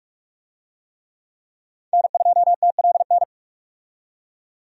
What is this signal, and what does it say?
Morse code, heard as a keyed tone, about 46 wpm, 700 Hz. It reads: N2TLN
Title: USA Call Signs - 5 Characters - 45wpm